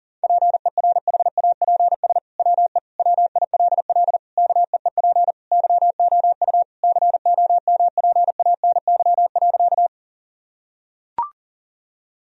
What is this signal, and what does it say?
Morse code, heard as a keyed tone, about 40 wpm, 700 Hz. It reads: PERHAPS WE WILL KEEP YOU COMPANY.
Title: Sentences from Top 1000 Words - Review Part 3 - 40wpm